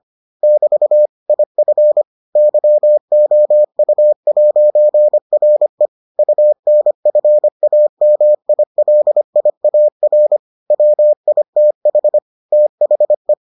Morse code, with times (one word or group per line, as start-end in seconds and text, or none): 0.4-1.1 <BT>
1.3-2.0 IF
2.4-5.9 YOU'RE
6.2-10.4 UNFAMILIAR
10.7-12.2 WITH
12.5-13.3 THE